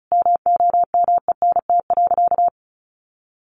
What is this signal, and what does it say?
Morse code, heard as a keyed tone, about 34 wpm, 700 Hz. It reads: MOMENT.
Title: Sentences from Top 1400 Words 35wpm